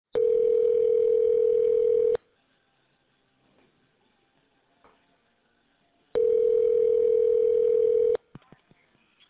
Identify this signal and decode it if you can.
Ringback tone (North America), 440 + 480 Hz.